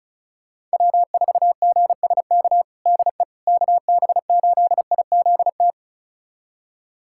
Morse code, heard as a keyed tone, about 36 wpm, 700 Hz. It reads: W4GSK DE KB8IZT
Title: USA Callsign DE Callsign 35wpm